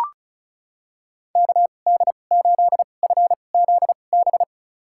Morse code, heard as a keyed tone, about 34 wpm, spaced as for 25 wpm, 700 Hz. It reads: KD8FZB
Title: USA Call Signs - 6 Characters - 2x Character Spacing - 35wpm